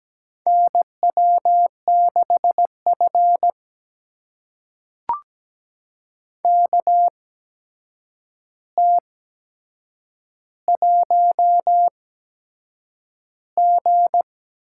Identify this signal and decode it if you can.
Morse code, heard as a keyed tone, about 17 wpm, spaced as for 7 wpm, 700 Hz. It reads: NW6F KT1G
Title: USA Call Signs - 4 Characters - 8x Character Spacing - 17wpm